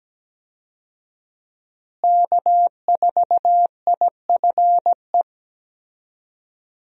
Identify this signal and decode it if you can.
Morse code, heard as a keyed tone, about 17 wpm, 700 Hz. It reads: K4IFE
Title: USA Call Signs - 5 Characters - Repeated 2x - 17wpm